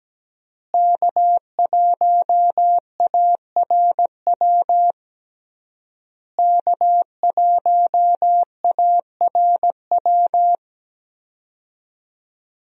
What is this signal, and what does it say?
Morse code, heard as a keyed tone, about 17 wpm, 700 Hz. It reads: K1ARW K1ARW
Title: USA Call Signs - 5 Characters - Repeated 3x - 17wpm